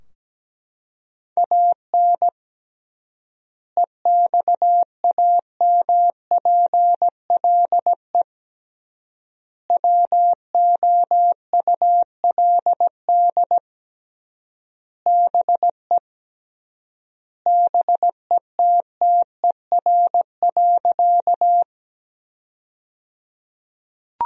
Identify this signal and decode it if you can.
Morse code, heard as a keyed tone, about 17 wpm, 700 Hz. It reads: AN EXAMPLE WOULD BE BETTER.
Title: 3x Word-Spacing - Sentences from Top 400 Words - 17wpm